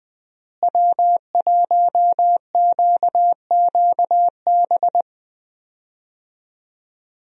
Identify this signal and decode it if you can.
Morse code, heard as a keyed tone, about 20 wpm, 700 Hz. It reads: W1QQB
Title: USA Call Signs - 5 Characters - Repeated 2x - 20wpm